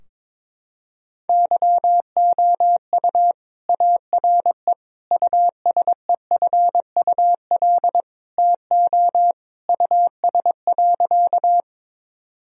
Morse code, heard as a keyed tone, about 22 wpm, 700 Hz. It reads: YOU ARE USEFUL TO US.
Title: Sentences from Top 1000 Words - Review Part 4 - 22wpm